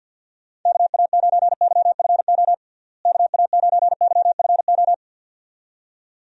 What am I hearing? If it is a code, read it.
Morse code, heard as a keyed tone, about 46 wpm, 700 Hz. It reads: KA9YFO KA9YFO
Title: USA Call Signs - 6 Characters - Repeated 2x - 50wpm